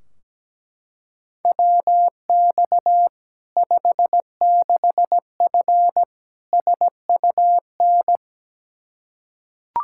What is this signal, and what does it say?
Morse code, heard as a keyed tone, about 17 wpm, 700 Hz. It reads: WX 56F SUN
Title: Practice QSO - Exchange 2 - Info - 17wpm